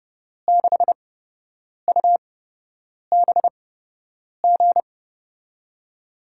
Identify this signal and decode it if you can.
Morse code, heard as a keyed tone, about 30 wpm, spaced as for 8 wpm, 700 Hz. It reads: 6UBG
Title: USA Call Signs - 8x Character Spacing - 30wpm